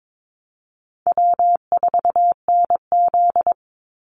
Morse code, heard as a keyed tone, about 22 wpm, 700 Hz. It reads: W4NZ
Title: Easy Call Signs Repeated 3x - 22wpm